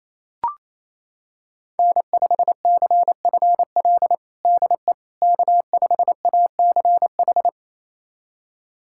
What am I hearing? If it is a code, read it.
Morse code, heard as a keyed tone, about 28 wpm, 700 Hz. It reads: N5CFL DE K5ACH
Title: USA Callsign DE Callsign - 5 Characters - 28wpm